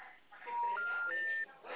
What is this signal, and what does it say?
Special information tone, with speech.
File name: out- -2016- - - .19030